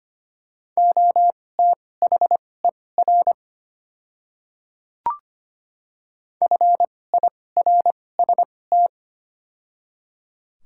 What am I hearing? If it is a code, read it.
Morse code, heard as a keyed tone, about 25 wpm, spaced as for 18 wpm, 700 Hz. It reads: OTHER FIRST
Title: 2x Character Spacing - Top 100 Words - 25wpm